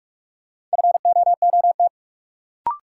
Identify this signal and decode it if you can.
Morse code, heard as a keyed tone, about 42 wpm, 700 Hz. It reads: FOOT
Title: Sentences from Top 500 Words - New Word Drill - 45wpm